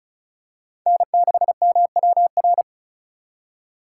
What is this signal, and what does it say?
Morse code, heard as a keyed tone, about 36 wpm, 700 Hz. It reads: N6MWR
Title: USA Call Signs - 5 Characters - 3x Character Spacing - 35wpm